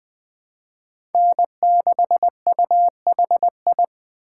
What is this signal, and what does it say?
Morse code, heard as a keyed tone, about 20 wpm, 700 Hz. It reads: N6UHI
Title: USA Call Signs - 5 Characters - 2x Character Spacing - 20wpm